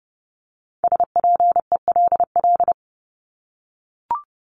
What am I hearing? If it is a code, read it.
Morse code, heard as a keyed tone, about 30 wpm, 700 Hz. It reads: SPELL
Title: Top 500 Words Repeated 30wpm